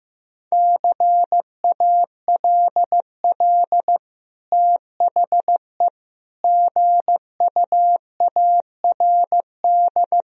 Morse code, heard as a keyed tone, about 15 wpm, 700 Hz. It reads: CALL THE GUARD
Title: Sentences from Top 1000 Words - Review Part 4 - 15wpm